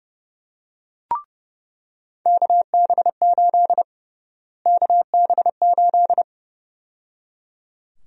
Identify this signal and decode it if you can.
Morse code, heard as a keyed tone, about 30 wpm, 700 Hz. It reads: KB8 KB8
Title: USA Call Sign Prefixes - Repeated 2x - 30wpm